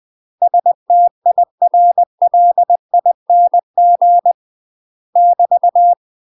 Morse code, heard as a keyed tone, about 20 wpm, 700 Hz. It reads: STIRLING <BT>
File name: UK MASTER QSOs TRAINING / 20WPM / UK MASTER QSOs 0065